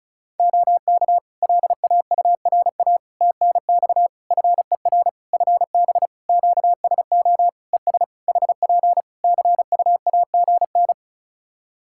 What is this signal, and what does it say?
Morse code, heard as a keyed tone, about 36 wpm, 700 Hz. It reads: OK LAURA TNX FER FB QSO ES HP CUAGN
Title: K1USN SST QSO Practice 35wpm